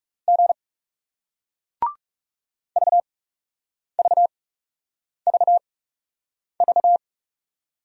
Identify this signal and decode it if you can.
Morse code, heard as a keyed tone, about 40 wpm, spaced as for 8 wpm, 700 Hz. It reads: G VVVV